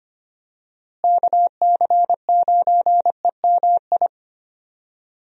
Morse code, heard as a keyed tone, about 25 wpm, 700 Hz. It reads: KC9EMI
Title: USA Call Signs - 6 Characters - Repeated 2x - 25wpm